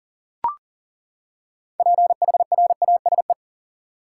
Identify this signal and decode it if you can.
Morse code, heard as a keyed tone, about 40 wpm, 700 Hz. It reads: PHRASE
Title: Six Letter Words - 40wpm